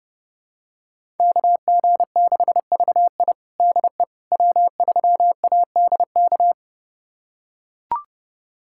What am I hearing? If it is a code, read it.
Morse code, heard as a keyed tone, about 32 wpm, 700 Hz. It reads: KG6VI DE W3ADK
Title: USA Callsign DE Callsign - 5 Characters - 30wpm